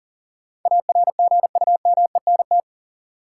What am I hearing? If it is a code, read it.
Morse code, heard as a keyed tone, about 39 wpm, 700 Hz. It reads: ARGUMENT
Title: Sentences from Top 1400 Words - New Word Drill Alt - 40wpm